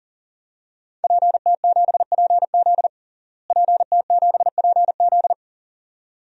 Morse code, heard as a keyed tone, about 40 wpm, 700 Hz. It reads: PT7PZ PT7PZ
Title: DXCC-100 Call Signs - 5 Characters - Repeated 2x - 40wpm